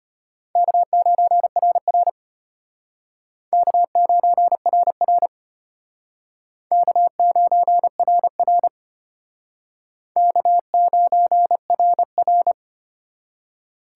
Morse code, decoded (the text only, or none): K9RR K9RR K9RR K9RR